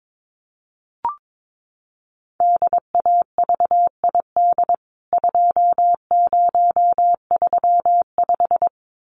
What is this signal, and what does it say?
Morse code, heard as a keyed tone, about 22 wpm, 700 Hz. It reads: DAVID 2035
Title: CWT Exchange 22wpm